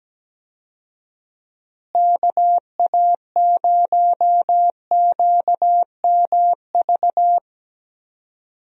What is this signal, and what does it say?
Morse code, heard as a keyed tone, about 17 wpm, 700 Hz. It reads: KA0QMV
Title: USA Call Signs - 6 Characters - Repeated 3x - 17wpm